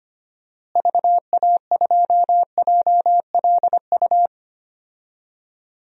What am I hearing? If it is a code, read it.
Morse code, heard as a keyed tone, about 25 wpm, 700 Hz. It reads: VA2JLU
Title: DXCC-100 Call Signs - 6 Characters - Repeated 3x - 25wpm